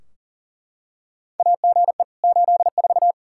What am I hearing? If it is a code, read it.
Morse code, heard as a keyed tone, about 40 wpm, 700 Hz. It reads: AGE 84
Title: Practice QSO - Exchange 3 - More Info - 40wpm